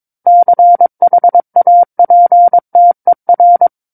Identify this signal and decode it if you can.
Morse code, heard as a keyed tone, about 22 wpm, 700 Hz. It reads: CHAPTER
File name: Book One / 13-Chapter 13 - 22 WPM